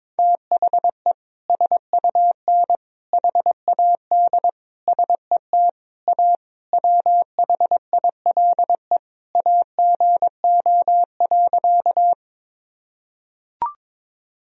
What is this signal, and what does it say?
Morse code, heard as a keyed tone, about 22 wpm, 700 Hz. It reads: THE SUN HAD SET A WHILE AGO.
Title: Sentences from Top 1000 Words - Review Part 1 - 22wpm